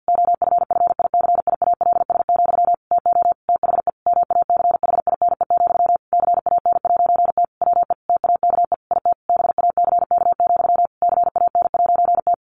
Morse code, heard as a keyed tone, about 48 wpm, 700 Hz. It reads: OFFICIALS, TO THE MACHINE, CAN'T WE TAKE IT BACK, CAN'T
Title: A Sound Of Thunder 50wpm